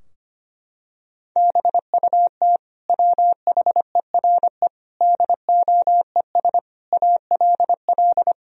Morse code, heard as a keyed tone, about 25 wpm, 700 Hz. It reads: BUT WHERE DOES ALL
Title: Sentences from Top 1000 Words - Review Part 2 - 25wpm